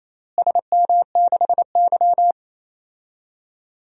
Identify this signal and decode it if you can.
Morse code, heard as a keyed tone, about 27 wpm, 700 Hz. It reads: SM6Y
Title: DXCC-100 Call Signs - 4 Characters - Repeated 3x - 28wpm